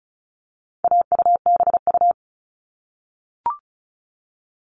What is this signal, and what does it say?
Morse code, heard as a keyed tone, about 34 wpm, 700 Hz. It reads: AUBU
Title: All Cut Numbers - 4-Digit 35wpm